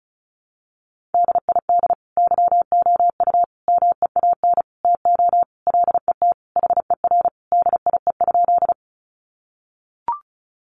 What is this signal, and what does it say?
Morse code, heard as a keyed tone, about 34 wpm, 700 Hz. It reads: DID YOU MEAN TO LET HER DIE?
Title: Sentences from Top 1300 Words 35wpm